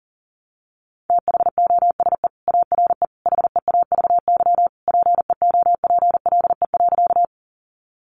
Morse code, heard as a keyed tone, about 40 wpm, 700 Hz. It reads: THOSE ARE HEAVY PEOPLE.
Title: Sentences from Top 1100 Words 40wpm